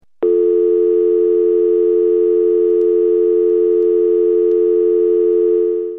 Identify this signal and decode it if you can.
Dial tone (North America), 350 + 440 Hz.
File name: House Phones/00 - Dial Tone